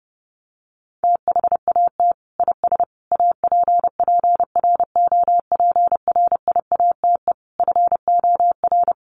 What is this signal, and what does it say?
Morse code, heard as a keyed tone, about 30 wpm, 700 Hz. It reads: THAT IS APPROPRIATE FOR
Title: Sentences from Top 900 Words 30wpm